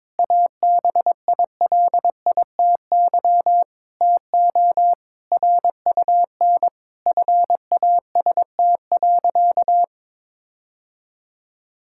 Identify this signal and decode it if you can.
Morse code, heard as a keyed tone, about 22 wpm, 700 Hz. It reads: ABILITY TO RUN FAST.
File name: practice / 22wpm / Sentences with 3-Word Phrases - 22wpm